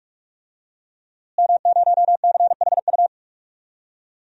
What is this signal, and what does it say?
Morse code, heard as a keyed tone, about 46 wpm, 700 Hz. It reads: M0CHU
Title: DXCC-100 Call Signs - 5 Characters - Repeated 2x - 45wpm